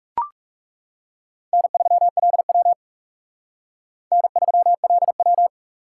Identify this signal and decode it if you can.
Morse code, heard as a keyed tone, about 40 wpm, 700 Hz. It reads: N3LW N3LW